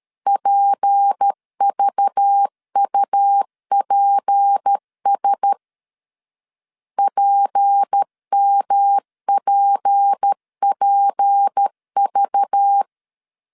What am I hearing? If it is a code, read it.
Morse code, heard as a keyed tone, about 13 wpm, 800 Hz. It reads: PVUPS PMPPV
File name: LEZIONI / 6 PKX0 / 6 P 12wpm 2033